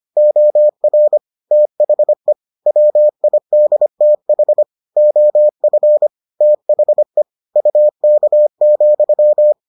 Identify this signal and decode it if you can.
Morse code, heard as a keyed tone, about 25 wpm, 600 Hz. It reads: OR THE WIDTH OF THE UK,